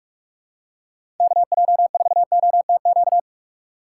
Morse code, heard as a keyed tone, about 46 wpm, 700 Hz. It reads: KJ4OTQ
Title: USA Call Signs - 6 Characters - 6x Character Spacing - 45wpm